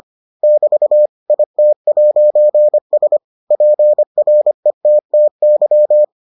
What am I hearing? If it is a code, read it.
Morse code, heard as a keyed tone, about 25 wpm, 600 Hz. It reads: <BT> IT'S PRETTY